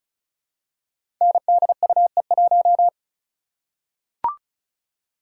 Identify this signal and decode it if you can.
Morse code, heard as a keyed tone, about 36 wpm, 700 Hz. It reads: NDUE1
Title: Code Groups - 5 Letter:Numbers - 35wpm